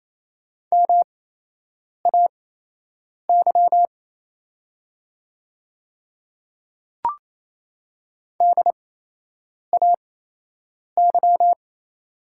Morse code, heard as a keyed tone, about 28 wpm, spaced as for 8 wpm, 700 Hz. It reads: MAY DAY
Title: 8x Character Spacing - Top 100 Words - 28wpm